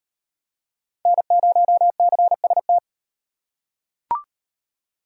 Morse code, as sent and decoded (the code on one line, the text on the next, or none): -. ----- -.-. ... -
N0CST